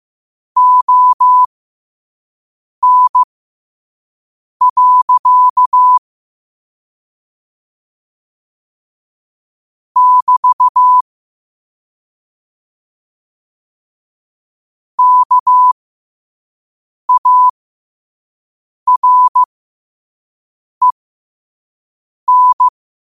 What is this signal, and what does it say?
Morse code, heard as a keyed tone, about 15 wpm, spaced as for 5 wpm, 1000 Hz. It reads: ON. <BT> KAREN
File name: qotdfiles / 5wpm